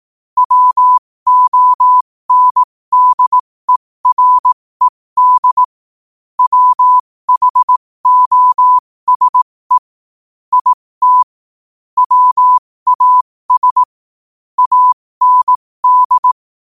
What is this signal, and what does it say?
Morse code, heard as a keyed tone, about 18 wpm, spaced as for 15 wpm, 1000 Hz. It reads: WONDERED WHOSE IT WAS AND
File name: qotdfiles / 15wpm